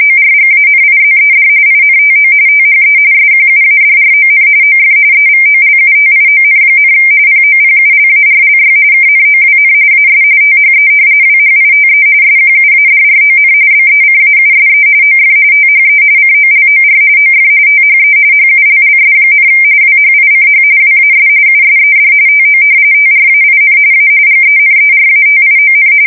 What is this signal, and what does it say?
RTTY, about 45.45 baud. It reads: ORGANIZED INTO THE SQUADRON OF EVOLUTION, POPULARLY KNOWN AS THE WHITE SQUADRON BECAUSE OF THE COLOR OF THE HULLS, WHICH WAS USED TO TRAIN A GENERATION
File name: fsk170 / usnavy3